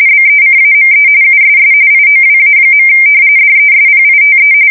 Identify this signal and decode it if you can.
RTTY, about 45.45 baud. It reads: BECAME USS IOWA. DESPITE
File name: fsk170 / battleships